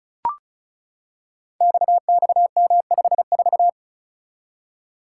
Morse code, heard as a keyed tone, about 36 wpm, 700 Hz. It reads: XXM54
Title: Code Groups - 5 Letter:Numbers - 35wpm